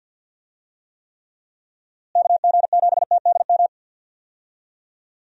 Morse code, heard as a keyed tone, about 50 wpm, 700 Hz. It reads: KG7TDM
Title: USA Call Signs - 6 Characters - Repeated 2x - 50wpm